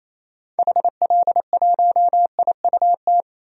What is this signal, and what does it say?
Morse code, heard as a keyed tone, about 28 wpm, 700 Hz. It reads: HL1IUT
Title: DXCC-100 Call Signs - 6 Characters - 28wpm